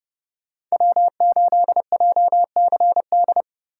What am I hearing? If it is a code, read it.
Morse code, heard as a keyed tone, about 33 wpm, 700 Hz. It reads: W8JCD